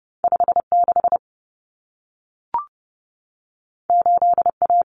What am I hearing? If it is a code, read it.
Morse code, heard as a keyed tone, about 30 wpm, 700 Hz. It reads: 56 8A
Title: Common Cut Numbers - 2-Digit 30wpm